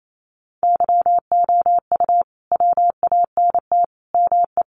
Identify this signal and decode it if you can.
Morse code, heard as a keyed tone, about 28 wpm, 700 Hz. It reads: YOU WANT ME
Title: Sets of 4 Words 28wpm